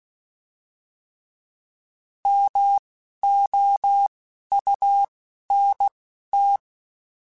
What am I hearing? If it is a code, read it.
Morse code, heard as a keyed tone, about 16 wpm, spaced as for 11 wpm, 800 Hz. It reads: MOUNT
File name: fromSMs / FromSMs mp3s / Cities@10